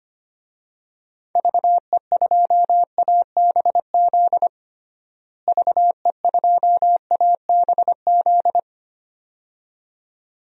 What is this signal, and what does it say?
Morse code, heard as a keyed tone, about 25 wpm, 700 Hz. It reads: VE2ABZ VE2ABZ
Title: DXCC-100 Call Signs - 6 Characters - Repeated 2x - 25wpm